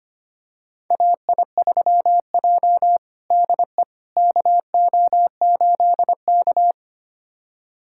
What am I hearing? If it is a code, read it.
Morse code, heard as a keyed tone, about 25 wpm, 700 Hz. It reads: AI3J DE KO8K
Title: USA Callsign DE Callsign - 4 Characters - 25wpm